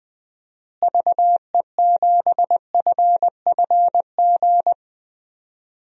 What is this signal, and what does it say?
Morse code, heard as a keyed tone, about 20 wpm, 700 Hz. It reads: VE7FFG
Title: DXCC-100 Call Signs - 6 Characters - Repeated 2x - 20wpm